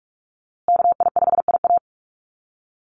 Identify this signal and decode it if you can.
Morse code, heard as a keyed tone, about 44 wpm, 700 Hz. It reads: KI5IA
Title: USA Call Signs Repeated 3x 45wpm